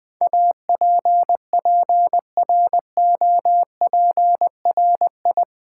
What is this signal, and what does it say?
Morse code, heard as a keyed tone, about 20 wpm, 700 Hz. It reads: APPROPRI
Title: Build A Word - 4 Letters At A Time - 11-Letter Words - 20wpm